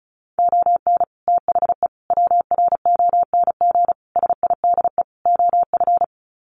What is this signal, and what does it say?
Morse code, heard as a keyed tone, about 34 wpm, 700 Hz. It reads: ON THE WRONG SIDE OF
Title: Sets of 5 Words Encore 35wpm